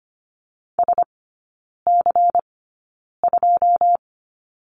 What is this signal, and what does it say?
Morse code, heard as a keyed tone, about 25 wpm, spaced as for 9 wpm, 700 Hz. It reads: SC2
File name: Lesson 17e - Letter u - Three Characters 25wpm